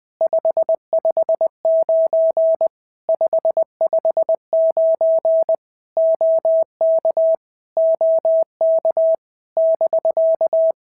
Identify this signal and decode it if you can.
Morse code, heard as a keyed tone, about 19 wpm, 650 Hz. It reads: 559 559 OK OK <BK>
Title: POTA QSO Practice - 20wpm